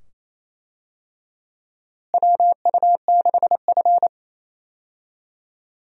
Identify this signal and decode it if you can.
Morse code, heard as a keyed tone, about 28 wpm, 700 Hz. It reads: WU6F
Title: USA Call Signs - 4 Characters - Repeated 2x - 28wpm